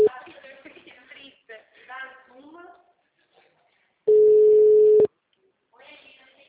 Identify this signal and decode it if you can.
Ringback tone (Europe, 425 Hz), with speech.
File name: out- -2012- - - .228